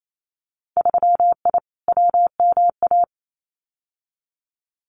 Morse code, heard as a keyed tone, about 28 wpm, 700 Hz. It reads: 3I WMA
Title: Winter Field Day Exchanges Encore 28wpm